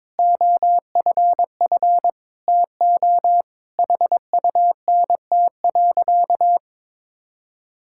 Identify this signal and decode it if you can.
Morse code, heard as a keyed tone, about 22 wpm, 700 Hz. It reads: OFF TO HUNT.
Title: Sentences from Top 1600 Words 22wpm